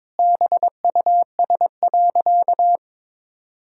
Morse code, heard as a keyed tone, about 22 wpm, 700 Hz. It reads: BUS.
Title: Sentences from Top 1000 Words - Review Part 6 - 22wpm